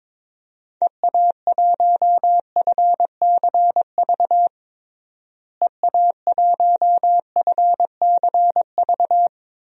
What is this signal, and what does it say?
Morse code, heard as a keyed tone, about 22 wpm, 700 Hz. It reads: EA1FCV EA1FCV
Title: DXCC-100 Call Signs - 6 Characters - Repeated 2x - 22wpm